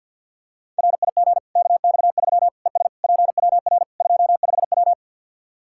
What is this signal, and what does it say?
Morse code, heard as a keyed tone, about 52 wpm, 700 Hz. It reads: RIG KX3 ES PWR 25W
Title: Practice QSO - Exchange 2 - Info - 50wpm